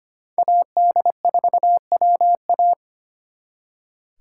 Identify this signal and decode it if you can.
Morse code, heard as a keyed tone, about 32 wpm, 700 Hz. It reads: AD4WA